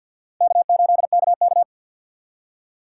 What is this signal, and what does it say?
Morse code, heard as a keyed tone, about 62 wpm, 700 Hz. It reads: K8KK
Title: Speed Racing - USA 1x2 Call Signs 50wpm